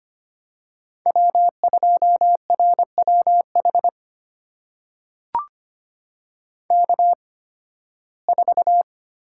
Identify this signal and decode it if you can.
Morse code, heard as a keyed tone, about 25 wpm, spaced as for 16 wpm, 700 Hz. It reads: W2RWH K4
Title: USA Call Signs - 5 Characters - 8x Character Spacing - 25wpm